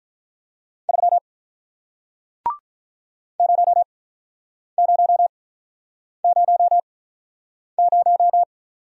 Morse code, heard as a keyed tone, about 46 wpm, spaced as for 9 wpm, 700 Hz. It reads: 3 0000